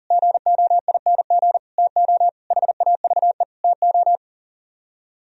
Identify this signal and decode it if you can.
Morse code, heard as a keyed tone, about 49 wpm, 700 Hz. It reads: GOING TO HAVE TO
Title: Speed-Racing - Sets of 5 Words 40wpm